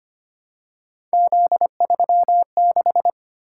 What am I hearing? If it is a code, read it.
Morse code, heard as a keyed tone, about 25 wpm, 700 Hz. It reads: Z36
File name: Three Letter-Number Combinations - 25wpm